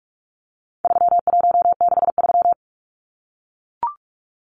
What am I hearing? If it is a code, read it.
Morse code, heard as a keyed tone, about 44 wpm, 700 Hz. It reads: 3163
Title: Numbers - 4-Digit 45wpm